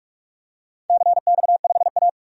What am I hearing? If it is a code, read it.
Morse code, heard as a keyed tone, about 44 wpm, 700 Hz. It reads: CX5A
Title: DXCC-100 Call Signs - 4 Characters - 45wpm